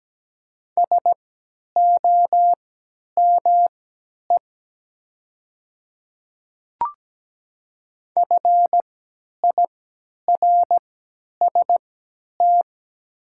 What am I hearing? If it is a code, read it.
Morse code, heard as a keyed tone, about 17 wpm, spaced as for 10 wpm, 700 Hz. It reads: SOME FIRST
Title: 3x Character Spacing - Top 100 Words - 17wpm